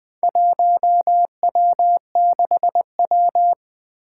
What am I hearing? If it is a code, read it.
Morse code, heard as a keyed tone, about 20 wpm, 700 Hz. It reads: 1W6W
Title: Four Challenging Letter-Number Combinations 20wpm